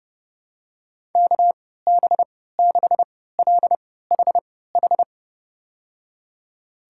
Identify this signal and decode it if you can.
Morse code, heard as a keyed tone, about 30 wpm, spaced as for 17 wpm, 700 Hz. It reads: KB6LHH
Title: USA Call Signs - 6 Characters - 3x Character Spacing - 30wpm